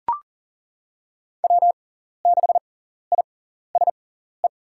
Morse code, heard as a keyed tone, about 39 wpm, spaced as for 14 wpm, 700 Hz. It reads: W6ISE
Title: USA Call Signs - 5 Characters - 6x Character Spacing - 40wpm